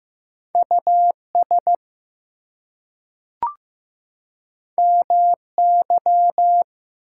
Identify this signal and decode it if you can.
Morse code, heard as a keyed tone, about 15 wpm, 700 Hz. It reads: US MY
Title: Two Letter Words - 15wpm